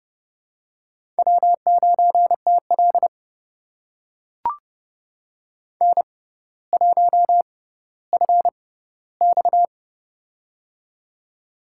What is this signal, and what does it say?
Morse code, heard as a keyed tone, about 30 wpm, spaced as for 15 wpm, 700 Hz. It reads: W9TL N1FX
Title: USA Call Signs - 4 Characters - 6x Character Spacing - 30wpm